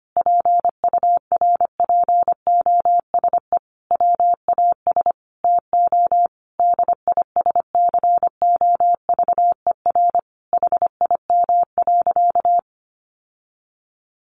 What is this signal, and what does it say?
Morse code, heard as a keyed tone, about 25 wpm, 700 Hz. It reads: PURPOSE WAS TO DISCOVER HIM.
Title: Sentences from Top 1100 Words 25wpm